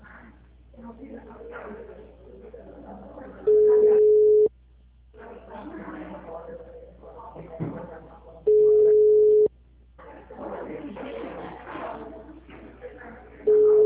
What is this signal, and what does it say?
Ringback tone (Europe, 425 Hz), with speech.